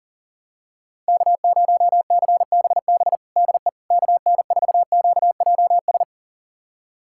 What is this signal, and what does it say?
Morse code, heard as a keyed tone, about 40 wpm, 700 Hz. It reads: K0CBB DE KN4QJS
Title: USA Callsign DE Callsign 40wpm